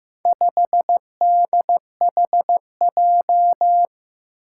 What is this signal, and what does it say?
Morse code, heard as a keyed tone, about 15 wpm, 700 Hz. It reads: 5DHJ
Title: Four Challenging Letter-Number Combinations 15wpm